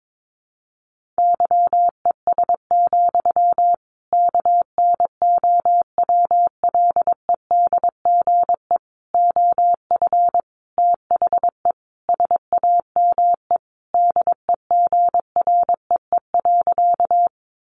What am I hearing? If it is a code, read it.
Morse code, heard as a keyed tone, about 22 wpm, 700 Hz. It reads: YES, KNOWLEDGE OF THE SAME DEGREE.
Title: Sentences from Top 1500 Words 22wpm